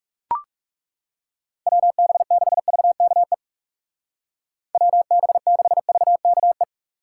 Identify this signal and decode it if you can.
Morse code, heard as a keyed tone, about 38 wpm, 700 Hz. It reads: WB6VKE WB6VKE